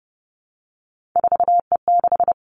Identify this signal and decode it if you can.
Morse code, heard as a keyed tone, about 30 wpm, 700 Hz. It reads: 4E6
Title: Three Challenging Letter-Number Combinations 30wpm